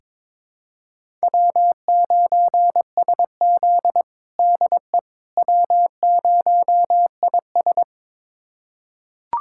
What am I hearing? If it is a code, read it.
Morse code, heard as a keyed tone, about 22 wpm, 700 Hz. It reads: W9SZ DE W0IS
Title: USA Callsign DE Callsign - 4 Characters - 22wpm